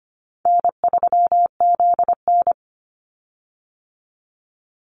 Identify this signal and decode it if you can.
Morse code, heard as a keyed tone, about 25 wpm, 700 Hz. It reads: N3ZN
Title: Call Signs - 25wpm